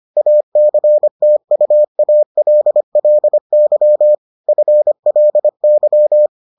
Morse code, heard as a keyed tone, about 25 wpm, 600 Hz. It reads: ACTUALLY FLY